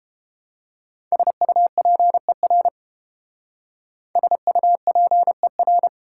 Morse code, heard as a keyed tone, about 30 wpm, 700 Hz. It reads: SUPER SUPER